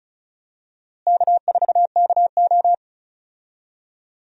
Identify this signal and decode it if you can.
Morse code, heard as a keyed tone, about 36 wpm, 700 Hz. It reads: K4KO
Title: USA Call Signs - 4 Characters - Repeated 2x - 35wpm